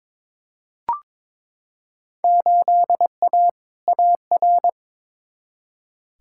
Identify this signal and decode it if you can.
Morse code, heard as a keyed tone, about 22 wpm, 700 Hz. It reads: 8A AR
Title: Field Day Exchanges Encore - 22wpm